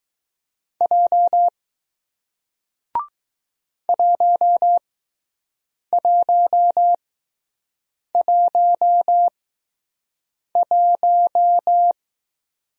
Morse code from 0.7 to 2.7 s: J

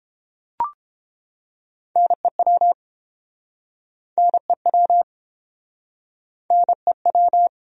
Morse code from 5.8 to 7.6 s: NEW

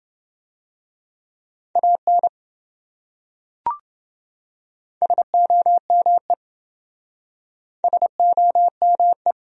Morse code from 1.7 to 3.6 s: AN

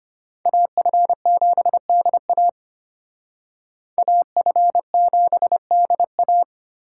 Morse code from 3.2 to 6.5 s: AF7DA

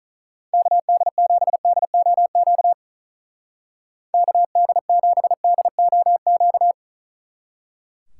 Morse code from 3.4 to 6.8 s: KD7DOQ

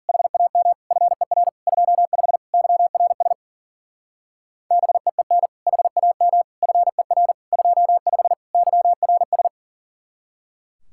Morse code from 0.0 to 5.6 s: HAM FER 25 YRS BEEN

